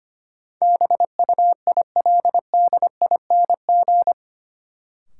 Morse code, heard as a keyed tone, about 25 wpm, 700 Hz. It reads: BUILDING
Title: Sentences from Top 700 Words - New Word Drill - 25wpm